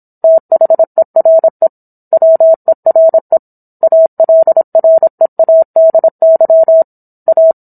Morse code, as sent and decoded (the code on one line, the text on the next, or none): - .... .-. / .-- . .-. / .- .-.. .-. .- -.. -.-- / .-
THERE WERE ALREADY A